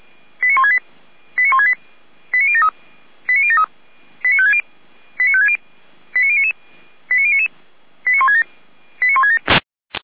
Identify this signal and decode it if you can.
Five-tone selective call (ZVEI), heard as quick runs of five tones, tones of about 70 ms. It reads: 89168 89168 89073 89073 89560 89560 89099 89099 89167 89167